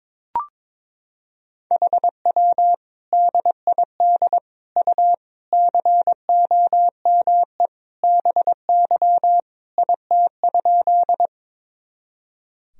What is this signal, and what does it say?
Morse code, heard as a keyed tone, about 22 wpm, 700 Hz. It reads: HW DID U COME BY IT?
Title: Sentences from Top 100 Words and Abbreviations 22wpm